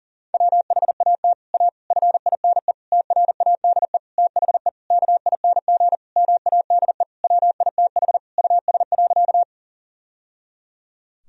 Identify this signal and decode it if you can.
Morse code, heard as a keyed tone, about 37 wpm, 690 Hz. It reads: WHAT A FINE TRADE THE KING MADE WITH US.
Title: Sentences from Top 1000 Words - Review Part 4 - 40wpm